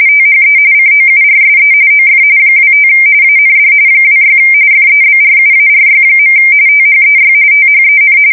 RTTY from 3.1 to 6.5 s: AND 26 FEBRUARY.